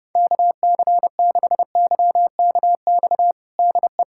Morse code, heard as a keyed tone, about 30 wpm, 700 Hz. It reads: KC6YKX DE
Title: USA Callsign DE Callsign - 6 Characters - 30wpm